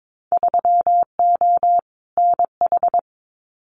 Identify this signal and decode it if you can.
Morse code, heard as a keyed tone, about 22 wpm, 700 Hz. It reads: 3O NH
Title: Winter Field Day Exchanges Encore 22wpm